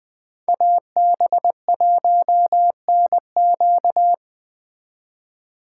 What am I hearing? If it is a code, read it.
Morse code, heard as a keyed tone, about 20 wpm, 700 Hz. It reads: AB1NQ
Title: Speed Racing - USA 2x2 Call Signs 17wpm